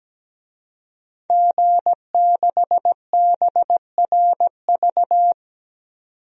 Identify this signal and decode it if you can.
Morse code, heard as a keyed tone, about 17 wpm, 700 Hz. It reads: G6BRV
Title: DXCC-100 Call Signs - 5 Characters - 17wpm